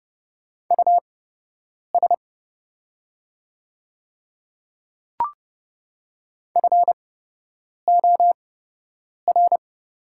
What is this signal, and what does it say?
Morse code, heard as a keyed tone, about 30 wpm, spaced as for 8 wpm, 700 Hz. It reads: US FOR
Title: 8x Character Spacing - Top 300 Words - 30wpm